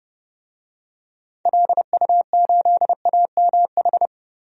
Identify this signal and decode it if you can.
Morse code, heard as a keyed tone, about 30 wpm, 700 Hz. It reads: LU8AMH
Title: DXCC-100 Call Signs - 6 Characters - Repeated 2x - 30wpm